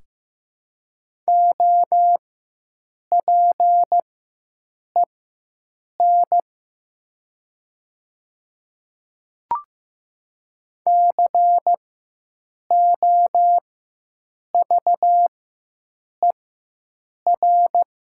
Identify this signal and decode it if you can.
Morse code, heard as a keyed tone, about 15 wpm, spaced as for 7 wpm, 700 Hz. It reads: OPEN COVER
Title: 4x Character Spacing - Top 300 Words - 15wpm